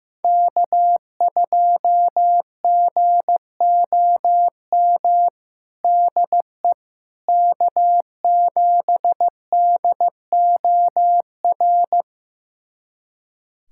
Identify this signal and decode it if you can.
Morse code, heard as a keyed tone, about 15 wpm, 700 Hz. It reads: K2GOM DE K7DOR
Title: USA Callsign DE Callsign - 5 Characters - 15wpm